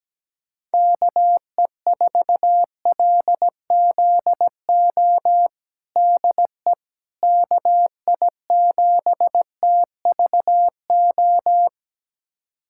Morse code, heard as a keyed tone, about 17 wpm, 700 Hz. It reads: KE4LZO DE KI7TVO
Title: USA Callsign DE Callsign - 6 Characters - 17wpm